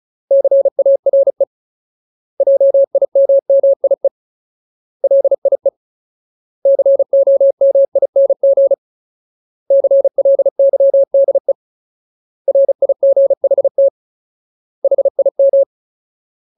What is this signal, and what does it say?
Morse code, heard as a keyed tone, about 34 wpm, 550 Hz. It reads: CARE JIMMIE LIE COMING CLYDE RIGHT HIM
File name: Ultimate Word List / 35WPM Ultimate